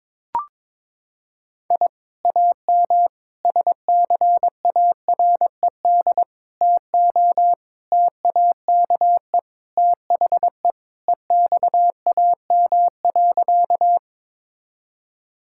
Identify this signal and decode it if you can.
Morse code, heard as a keyed tone, about 22 wpm, 700 Hz. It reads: I AM SCARED TO TAKE THE EXAM.
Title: Sentences from Top 1000 Words - Review Part 5 - 22wpm